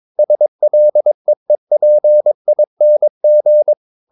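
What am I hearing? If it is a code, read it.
Morse code, heard as a keyed tone, about 22 wpm, 600 Hz. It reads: SLEEPING
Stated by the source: Morse code practice at 22 wpm, sending text from an English novel.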